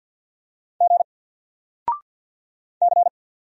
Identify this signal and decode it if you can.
Morse code, heard as a keyed tone, about 50 wpm, 700 Hz. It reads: G C
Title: Single Character (Letter:Number:Punctuation:Prosign) 50wpm